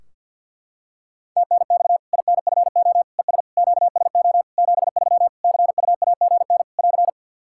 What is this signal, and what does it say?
Morse code, heard as a keyed tone, about 50 wpm, 700 Hz. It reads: TNX INFO ES QSO 73 CUAGN <AR>
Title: K1USN SST QSO Practice 50wpm